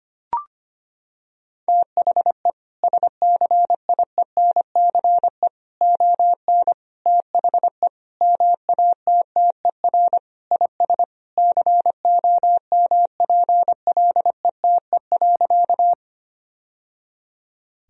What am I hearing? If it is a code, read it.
Morse code, heard as a keyed tone, about 25 wpm, 700 Hz. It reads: THE SCIENCE ON THE MATTER IS COMPLETE.
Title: Sentences from Top 1000 Words - Review Part 2 - 25wpm